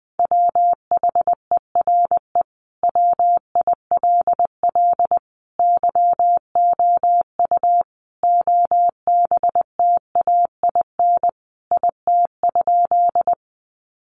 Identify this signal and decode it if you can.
Morse code, heard as a keyed tone, about 20 wpm, 700 Hz. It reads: WHERE WILL YOU OBTAIN IT?
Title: Sentences from Top 1300 Words 20wpm